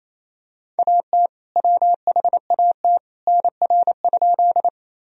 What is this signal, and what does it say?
Morse code, heard as a keyed tone, about 28 wpm, 700 Hz. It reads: AT WHAT NR?
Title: Sentences from Top 100 Words and Abbreviations 28wpm